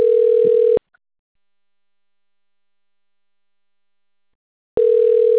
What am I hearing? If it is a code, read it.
Ringback tone (North America), 440 + 480 Hz.
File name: rg-1600-unknown- - - .90123